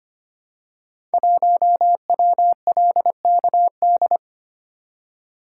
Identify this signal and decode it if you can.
Morse code, heard as a keyed tone, about 25 wpm, 700 Hz. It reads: 1WLKD
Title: Code Groups - 5 Letter:Numbers - 25wpm